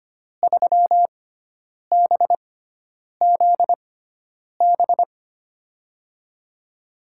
Morse code, heard as a keyed tone, about 25 wpm, spaced as for 9 wpm, 700 Hz. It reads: 3BZB
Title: USA Call Signs - 6x Character Spacing - 25wpm